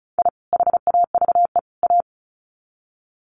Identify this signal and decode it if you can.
Morse code, heard as a keyed tone, about 34 wpm, 700 Hz. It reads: I HAVE A
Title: Sets of 3 Words 35wpm